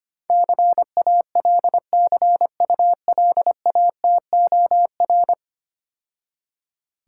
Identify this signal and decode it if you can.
Morse code, heard as a keyed tone, about 25 wpm, 700 Hz. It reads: CALCULATOR
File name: Easy Words - Speed-Racing with Hints - 25wpm